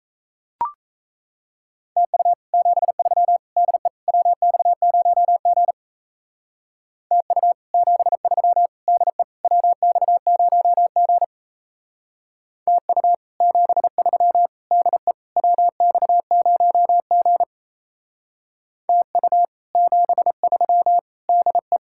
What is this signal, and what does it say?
Morse code, heard as a keyed tone, about 36 wpm, 700 Hz. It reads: TU 73 DE WX0G TU 73 DE WX0G TU 73 DE WX0G TU 73 DE